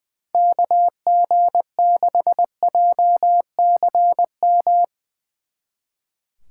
Morse code, heard as a keyed tone, about 20 wpm, 700 Hz. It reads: KG6JCM
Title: Speed Racing - USA 2x3 Call Signs 20wpm